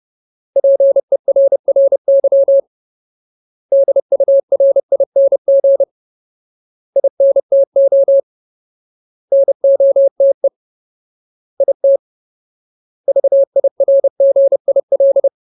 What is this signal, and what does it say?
Morse code, heard as a keyed tone, about 30 wpm, 550 Hz. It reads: PERRY DURING INTO NOTE IT VIRGIL
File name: Ultimate Word List / 30WPM Ultimate